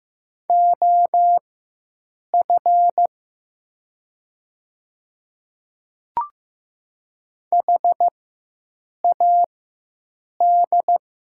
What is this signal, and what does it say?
Morse code, heard as a keyed tone, about 15 wpm, spaced as for 7 wpm, 700 Hz. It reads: OF HAD
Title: 4x Character Spacing - Top 200 Words - 15wpm